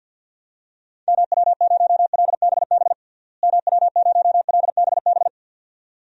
Morse code, heard as a keyed tone, about 51 wpm, 700 Hz. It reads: MW0LBB MW0LBB
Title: DXCC-100 Call Signs - 6 Characters - Repeated 2x - 50wpm